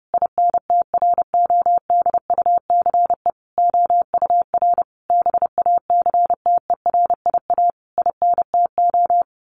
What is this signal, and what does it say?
Morse code, heard as a keyed tone, about 28 wpm, 700 Hz. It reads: INTRODUCE OUR BACTERIA INTO
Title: A Sound Of Thunder 30wpm